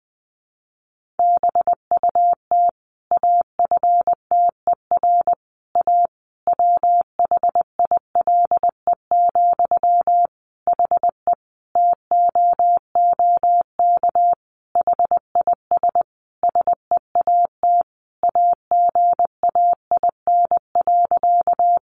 Morse code, heard as a keyed tone, about 20 wpm, 700 Hz. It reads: BUT AFTER A WHILE, HE TOOK HIS SEAT AGAIN.
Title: Sentences from Top 1500 Words 20wpm